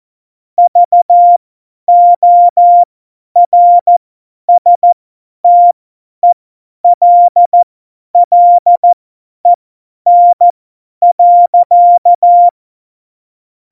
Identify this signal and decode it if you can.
Morse code, heard as a keyed tone, about 14 wpm, spaced as for 10 wpm, 700 Hz. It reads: VORSTELLEN.
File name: Morsekurs / Witze / Witz 10. Oktober 17, Tempo 50